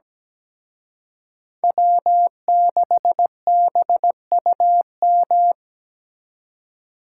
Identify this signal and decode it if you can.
Morse code, heard as a keyed tone, about 20 wpm, 700 Hz. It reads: W6BUM
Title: Speed Racing - USA Call Signs 17wpm